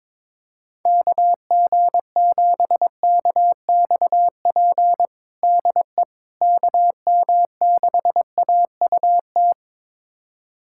Morse code, heard as a keyed tone, about 22 wpm, 700 Hz. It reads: KG7KXP DE KM6AUT
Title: USA Callsign DE Callsign - 6 Characters - 22wpm